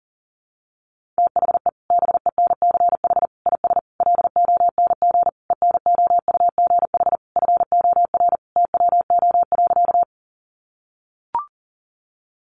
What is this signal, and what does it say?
Morse code, heard as a keyed tone, about 40 wpm, 700 Hz. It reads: THE BENCH IS LONG ENOUGH FOR TWO.
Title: Sentences from Top 1400 Words 40wpm